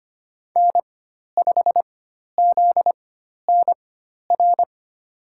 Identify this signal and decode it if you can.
Morse code, heard as a keyed tone, about 25 wpm, spaced as for 12 wpm, 700 Hz. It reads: N5ZNR
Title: USA Call Signs - 5 Characters - 4x Character Spacing - 25wpm